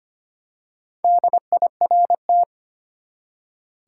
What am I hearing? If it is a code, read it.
Morse code, heard as a keyed tone, about 25 wpm, 700 Hz. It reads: DIRT
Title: Four Letter Words - 25wpm